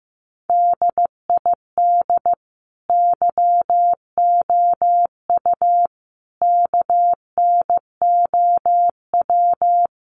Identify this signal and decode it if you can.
Morse code, heard as a keyed tone, about 15 wpm, 700 Hz. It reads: DID YOU KNOW
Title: Sentences from Top 100 Words 15wpm